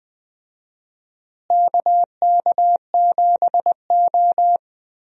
Morse code, heard as a keyed tone, about 20 wpm, 700 Hz. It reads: KK7O
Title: USA Call Signs - 4 Characters - Repeated 3x - 20wpm